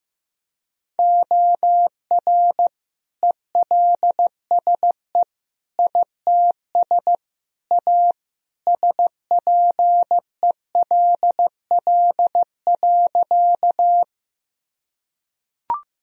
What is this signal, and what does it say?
Morse code, heard as a keyed tone, about 15 wpm, 700 Hz. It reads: OR ELSE ITS A SPELL.
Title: Sentences from Top 1000 Words - Review Part 1 - 15wpm